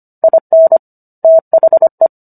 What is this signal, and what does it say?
Morse code, heard as a keyed tone, about 25 wpm, 650 Hz. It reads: IN THE